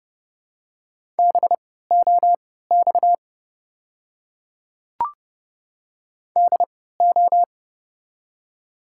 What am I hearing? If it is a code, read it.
Morse code, heard as a keyed tone, about 32 wpm, spaced as for 17 wpm, 700 Hz. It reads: BOX DO
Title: 3x Character Spacing - Top 500 Words - 30wpm